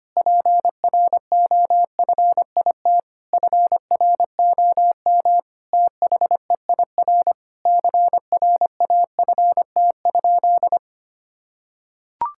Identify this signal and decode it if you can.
Morse code, heard as a keyed tone, about 25 wpm, 700 Hz. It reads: PROFIT FROM THEIR CRAFT?
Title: Sentences from Top 1000 Words - Review Part 6 - 25wpm